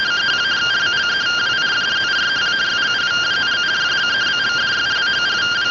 Two-tone teleprinter signal, about 170 Hz shift.
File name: findings / SAGITAL nb203 fr61 3mm.gen-11 rtty rx